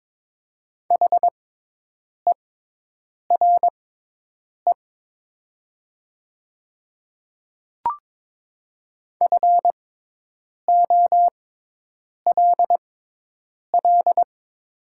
Morse code, heard as a keyed tone, about 22 wpm, spaced as for 8 wpm, 700 Hz. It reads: HERE FOLL
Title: 6x Character Spacing - Top 200 Words - 22wpm